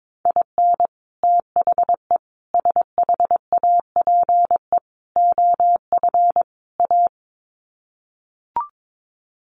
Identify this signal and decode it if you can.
Morse code, heard as a keyed tone, about 22 wpm, 700 Hz. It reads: IN THE SHAPE OF A
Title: Sets of 5 Words Encore 22wpm